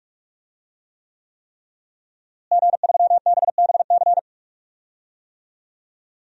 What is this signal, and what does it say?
Morse code, heard as a keyed tone, about 41 wpm, 700 Hz. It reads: G3BBC
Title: DXCC-100 Call Signs - 5 Characters - Repeated 3x - 45wpm